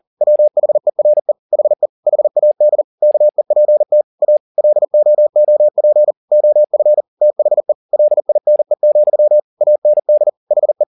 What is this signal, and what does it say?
Morse code, heard as a keyed tone, about 40 wpm, 600 Hz. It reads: WHERE HE HAD KEPT A LOOP OF THE LINE, AND HE